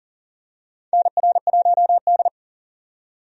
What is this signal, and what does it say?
Morse code, heard as a keyed tone, about 40 wpm, 700 Hz. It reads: NR1D
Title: USA Call Signs - 4 Characters - 40wpm